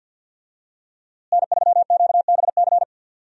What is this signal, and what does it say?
Morse code, heard as a keyed tone, about 52 wpm, 700 Hz. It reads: N3QBC
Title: USA Call Signs - 5 Characters - Repeated 2x - 50wpm